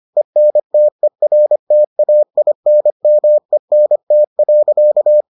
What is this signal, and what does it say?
Morse code, heard as a keyed tone, about 25 wpm, 600 Hz. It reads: ENTERTAINMENT.